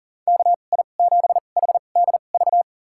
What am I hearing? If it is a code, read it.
Morse code, heard as a keyed tone, about 38 wpm, spaced as for 28 wpm, 700 Hz. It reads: KI7HDV
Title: USA Call Signs - 6 Characters - 2x Character Spacing - 40wpm